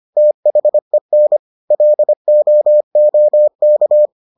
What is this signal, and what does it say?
Morse code, heard as a keyed tone, about 25 wpm, 600 Hz. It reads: THEN LOOK